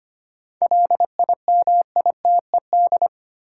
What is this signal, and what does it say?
Morse code, heard as a keyed tone, about 25 wpm, 700 Hz. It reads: LIMITED
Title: Sentences from Top 900 Words - New Word Drill - 25wpm